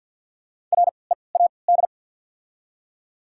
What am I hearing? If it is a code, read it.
Morse code, heard as a keyed tone, about 53 wpm, spaced as for 29 wpm, 700 Hz. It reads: READ
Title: 3x Character Spacing - Top 500 Words - 50wpm